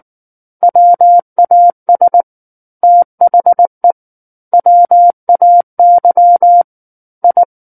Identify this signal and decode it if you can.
Morse code, heard as a keyed tone, about 19 wpm, 700 Hz. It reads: WAS THE WAY I
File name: Book One / 10-Chapter 10 - 19 WPM